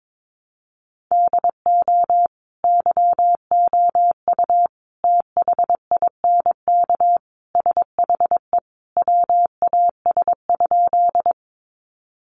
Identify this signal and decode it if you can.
Morse code, heard as a keyed tone, about 22 wpm, 700 Hz. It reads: DO YOU THINK SHE WAS?
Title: Sentences from Top 200 Words 22wpm